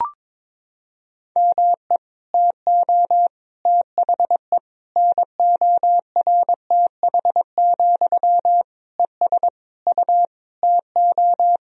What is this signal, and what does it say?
Morse code, heard as a keyed tone, about 22 wpm, 700 Hz. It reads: ME TO THE NORTH, ES U TO
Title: Sentences from Top 400 Words and Abbreviations 22wpm